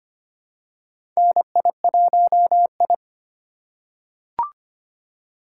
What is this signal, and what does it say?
Morse code, heard as a keyed tone, about 25 wpm, 700 Hz. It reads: NI1I
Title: USA Call Signs - 4 Characters - 3x Character Spacing - 25wpm